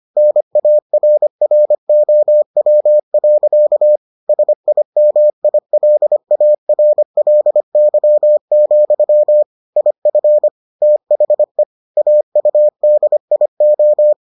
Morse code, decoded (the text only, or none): NARROW. SIMILARLY, IF THE AUDIO